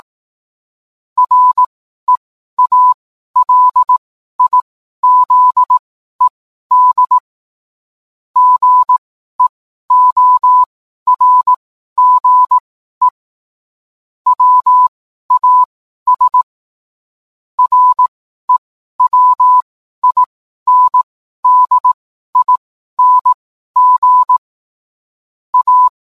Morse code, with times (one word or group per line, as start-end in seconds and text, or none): 1.2-7.2 REALIZED
8.4-13.1 GEORGE
14.3-16.4 WAS
17.6-24.4 REWINDING
25.5-25.9 A